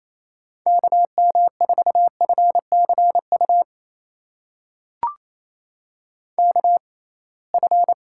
Morse code, heard as a keyed tone, about 28 wpm, spaced as for 21 wpm, 700 Hz. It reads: KM4FCU KF
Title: USA Call Signs - 6 Characters - 6x Character Spacing - 28wpm